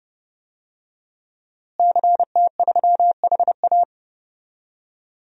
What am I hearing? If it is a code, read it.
Morse code, heard as a keyed tone, about 30 wpm, 700 Hz. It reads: CT3HA
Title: DXCC-100 Call Signs - 5 Characters - Repeated 2x - 30wpm